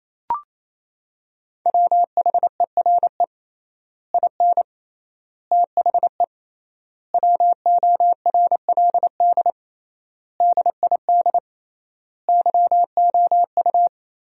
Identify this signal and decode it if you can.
Morse code, heard as a keyed tone, about 27 wpm, 700 Hz. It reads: WHERE IN THE WORLD DID YOU
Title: 3x Word-Spacing - Sentences from Top 200 Words - 28wpm